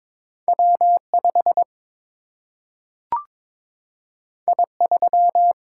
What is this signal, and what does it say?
Morse code, heard as a keyed tone, about 22 wpm, 700 Hz. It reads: W 5 I 3
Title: Single-Letter Single-Number 22wpm